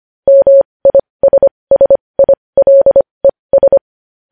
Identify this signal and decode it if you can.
Morse code, heard as a keyed tone, about 25 wpm, spaced as for 20 wpm, 550 Hz. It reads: MISSILES
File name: podcast / News headlines-1x-25x20